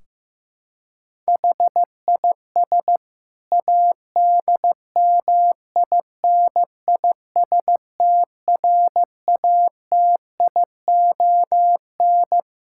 Morse code, heard as a keyed tone, about 15 wpm, 700 Hz. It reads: HIS ADMINISTRATION
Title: Sentences from Top 1600 Words 15wpm